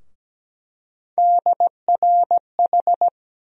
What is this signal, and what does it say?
Morse code, heard as a keyed tone, about 17 wpm, 700 Hz. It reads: DRH
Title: USA Call Sign Postfixes - 3 Letters 17wpm